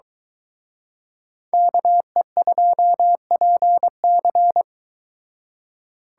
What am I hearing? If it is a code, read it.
Morse code, heard as a keyed tone, about 26 wpm, 700 Hz. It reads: KE2PC